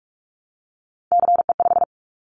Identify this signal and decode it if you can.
Morse code, heard as a keyed tone, about 46 wpm, 700 Hz. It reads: CE5
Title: Callsign Prefixes for Top 100 DXCC - 45wpm